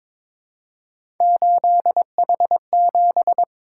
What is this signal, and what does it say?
Morse code, heard as a keyed tone, about 22 wpm, 700 Hz. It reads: 8H7
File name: practice / 22wpm / Three Letter-Number Combinations - 22wpm